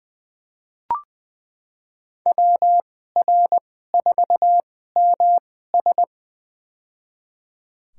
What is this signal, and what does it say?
Morse code, heard as a keyed tone, about 20 wpm, spaced as for 14 wpm, 700 Hz. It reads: WR4MS
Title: USA Call Signs - 2x Character Spacing - 20wpm